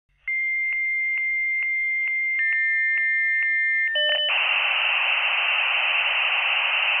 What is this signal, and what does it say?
Dial-up modem handshake.
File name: computer / libsounds.com - Modem connection, signal, interference 5